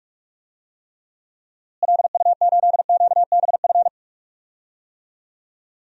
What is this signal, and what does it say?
Morse code, heard as a keyed tone, about 41 wpm, 700 Hz. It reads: LU8QBF
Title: DXCC-100 Call Signs - 6 Characters - Repeated 2x - 45wpm